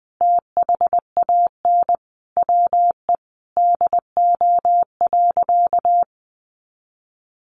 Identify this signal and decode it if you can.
Morse code, heard as a keyed tone, about 20 wpm, 700 Hz. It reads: THAN WE DO.
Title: Sentences from Top 100 Words 20wpm